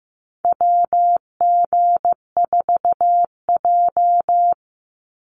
Morse code, heard as a keyed tone, about 15 wpm, 700 Hz. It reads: WG4J
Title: USA Call Signs 15wpm